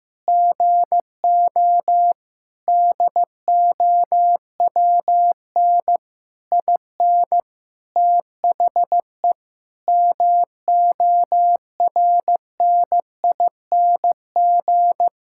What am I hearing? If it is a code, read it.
Morse code, heard as a keyed tone, about 15 wpm, 700 Hz. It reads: GO DOWN IN THE MORNING
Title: Sentences from Top 1600 Words 15wpm